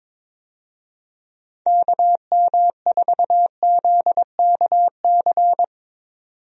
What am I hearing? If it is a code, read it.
Morse code, heard as a keyed tone, about 22 wpm, 700 Hz. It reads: KM4ZKC
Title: USA Call Signs - 6 Characters - 22wpm